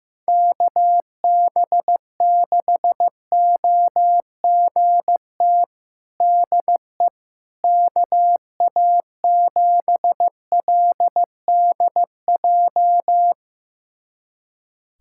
Morse code, heard as a keyed tone, about 15 wpm, 700 Hz. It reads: KB6OGT DE KA7LDJ
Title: USA Callsign DE Callsign - 6 Characters - 15wpm